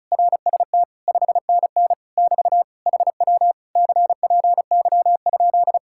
Morse code, heard as a keyed tone, about 37 wpm, 700 Hz. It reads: RST 5NN <BT> HW CPY?
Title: Speed-Racing Common QSO Phrases - 30wpm